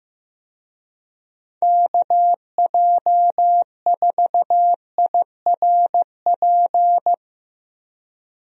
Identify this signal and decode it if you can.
Morse code, heard as a keyed tone, about 15 wpm, 700 Hz. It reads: KJ4IRP
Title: USA Call Signs - 6 Characters - Repeated 3x - 15wpm